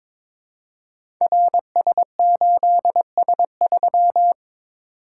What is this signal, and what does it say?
Morse code, heard as a keyed tone, about 22 wpm, 700 Hz. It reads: RS8S3
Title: Code Groups - 5 Letter:Numbers - 22wpm